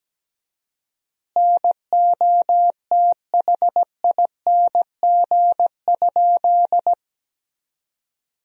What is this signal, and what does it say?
Morse code, heard as a keyed tone, about 17 wpm, 700 Hz. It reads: NOTHING?
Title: 4x Word-Spacing - Sentences from Top 500 Words - 17wpm